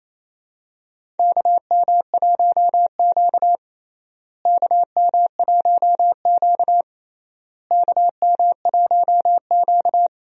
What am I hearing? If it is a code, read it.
Morse code, heard as a keyed tone, about 28 wpm, 700 Hz. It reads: KM1Q KM1Q KM1Q
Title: USA Call Signs - 4 Characters - Repeated 3x - 28wpm